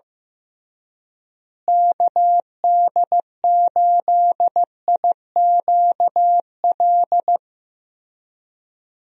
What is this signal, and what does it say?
Morse code, heard as a keyed tone, about 16 wpm, 700 Hz. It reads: KD8IQL
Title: USA Call Signs - 6 Characters - Repeated 3x - 15wpm